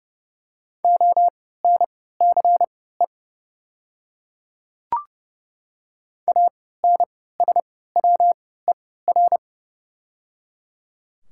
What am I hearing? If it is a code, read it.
Morse code, heard as a keyed tone, about 30 wpm, spaced as for 17 wpm, 690 Hz. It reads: ONCE ANSWER
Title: 3x Character Spacing - Top 300 Words - 30wpm